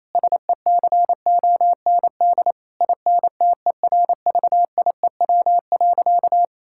Morse code, heard as a keyed tone, about 28 wpm, 700 Hz. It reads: SECOND INTERVIEW.
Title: Sentences from Top 1600 Words 28wpm